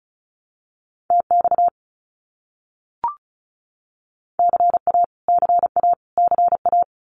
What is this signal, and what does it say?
Morse code, heard as a keyed tone, about 34 wpm, 700 Hz. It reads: TX CA CA CA
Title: US States Repeated 3x 35wpm